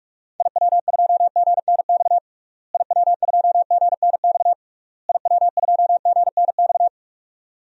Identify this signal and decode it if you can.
Morse code, heard as a keyed tone, about 42 wpm, 700 Hz. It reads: IW2GNX IW2GNX IW2GNX
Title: DXCC-100 Call Signs - 6 Characters - Repeated 3x - 45wpm